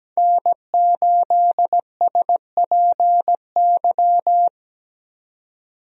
Morse code, heard as a keyed tone, about 17 wpm, 700 Hz. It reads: N8SPY
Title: USA Call Signs - 5 Characters - 17wpm